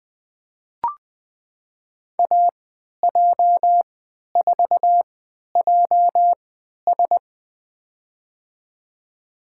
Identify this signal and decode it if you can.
Morse code, heard as a keyed tone, about 20 wpm, spaced as for 11 wpm, 700 Hz. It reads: AJ4JS
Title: USA Call Signs - 5 Characters - 3x Character Spacing - 20wpm